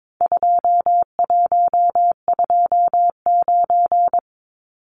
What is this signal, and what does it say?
Morse code, heard as a keyed tone, about 22 wpm, 700 Hz. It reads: 2129
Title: Numbers - 4-Digit 22wpm